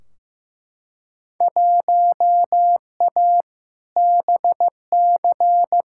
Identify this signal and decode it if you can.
Morse code, heard as a keyed tone, about 15 wpm, 700 Hz. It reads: 1A BC
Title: Field Day Exchanges - 15wpm